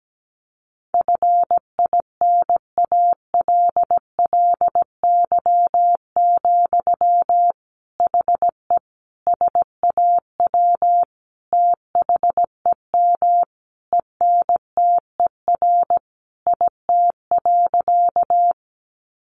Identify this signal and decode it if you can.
Morse code, heard as a keyed tone, about 17 wpm, 700 Hz. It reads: FINALLY, HE SAW THEM ENTER IT.
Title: Sentences from Top 1300 Words 17wpm